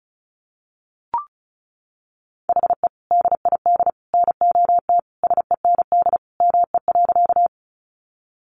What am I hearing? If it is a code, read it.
Morse code, heard as a keyed tone, about 34 wpm, 700 Hz. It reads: HE DID NOT SEND ME.
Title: Sentences from Top 1500 Words 35wpm